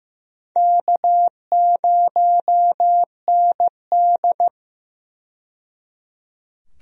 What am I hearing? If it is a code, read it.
Morse code, heard as a keyed tone, about 15 wpm, 700 Hz. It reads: K0ND
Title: USA Call Signs - 4 Characters - 15wpm